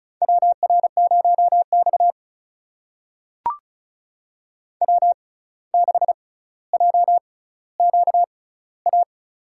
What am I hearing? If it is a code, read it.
Morse code, heard as a keyed tone, about 36 wpm, spaced as for 17 wpm, 700 Hz. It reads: WR0X W6JQA
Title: USA Call Signs - 6x Character Spacing - 35wpm